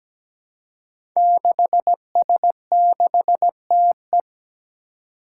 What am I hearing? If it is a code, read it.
Morse code, heard as a keyed tone, about 17 wpm, 700 Hz. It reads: 6S6TE
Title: Five Challenging Letter-Number Combinations 17wpm